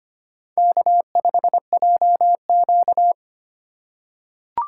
Morse code, heard as a keyed tone, about 25 wpm, 700 Hz. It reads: K5JQ
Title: USA Call Signs - 4 Characters - 2x Character Spacing - 25wpm